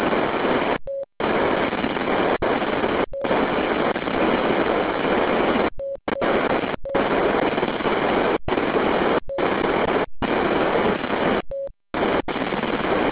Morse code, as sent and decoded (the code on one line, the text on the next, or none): none